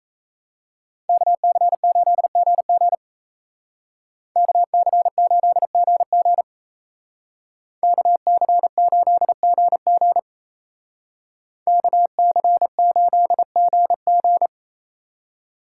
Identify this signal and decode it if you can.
Morse code, heard as a keyed tone, about 34 wpm, 700 Hz. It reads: KC8GG KC8GG KC8GG KC8GG
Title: Speed Racing - USA 2x2 Call Signs 28wpm